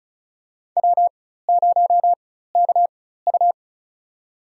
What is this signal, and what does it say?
Morse code, heard as a keyed tone, about 36 wpm, spaced as for 16 wpm, 700 Hz. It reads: W0KU
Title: USA Call Signs - 4 Characters - 4x Character Spacing - 35wpm